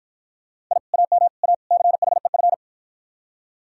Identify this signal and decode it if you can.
Morse code, heard as a keyed tone, about 52 wpm, 700 Hz. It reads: I AM A CHEF